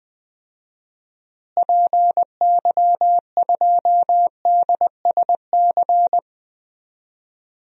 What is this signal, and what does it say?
Morse code, heard as a keyed tone, about 20 wpm, 700 Hz. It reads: PY2DSC
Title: DXCC-100 Call Signs - 6 Characters - Repeated 2x - 20wpm